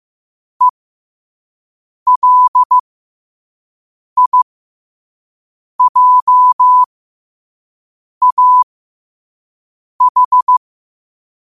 Morse code, heard as a keyed tone, about 15 wpm, spaced as for 5 wpm, 1000 Hz. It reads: ELIJAH
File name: qotdfiles / 5wpm